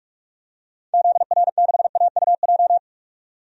Morse code, heard as a keyed tone, about 44 wpm, 690 Hz. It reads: ZR6AUJ
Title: DXCC-100 Call Signs - 6 Characters - 45wpm